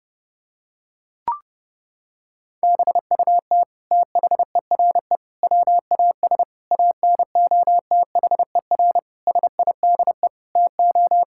Morse code, heard as a keyed tone, about 32 wpm, 700 Hz. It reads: BUT THERE WAS ANOTHER SIDE TO
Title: Sentences from Top 1600 Words 30wpm